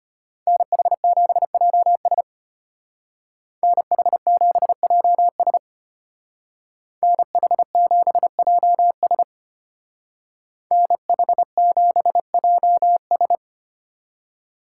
Morse code, heard as a keyed tone, about 33 wpm, 700 Hz. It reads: NH7JS NH7JS NH7JS NH7JS